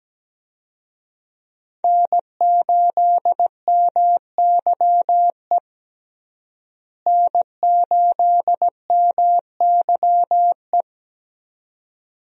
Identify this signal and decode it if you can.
Morse code, heard as a keyed tone, about 17 wpm, 700 Hz. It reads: N8MYE N8MYE
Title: USA Call Signs - 5 Characters - Repeated 2x - 17wpm